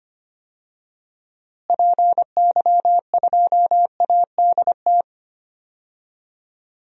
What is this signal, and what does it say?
Morse code, heard as a keyed tone, about 25 wpm, 700 Hz. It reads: PY2ADT
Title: DXCC-100 Call Signs - 6 Characters - Repeated 2x - 25wpm